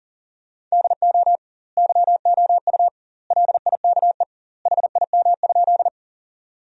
Morse code, heard as a keyed tone, about 40 wpm, 700 Hz. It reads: DO YOU LIKE HIM?
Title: 2x Word-Spacing - Sentences from Top 100 Words - 40wpm